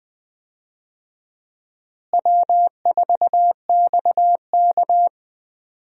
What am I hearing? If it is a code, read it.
Morse code, heard as a keyed tone, about 20 wpm, 700 Hz. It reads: W4XK
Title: USA Call Signs - 4 Characters - Repeated 3x - 20wpm